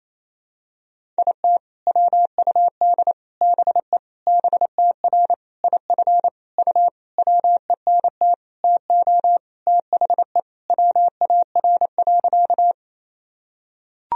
Morse code, heard as a keyed tone, about 28 wpm, 700 Hz. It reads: IT WUD BE BTR IF U WENT TO THE WAR.
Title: Sentences from Top 400 Words and Abbreviations 28wpm